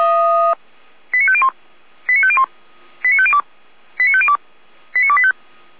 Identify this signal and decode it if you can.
Five-tone selective call (ZVEI), heard as quick runs of five tones, tones of about 70 ms. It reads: 89501 89501 89502 89502 89285
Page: /455/SDR/ZVEI/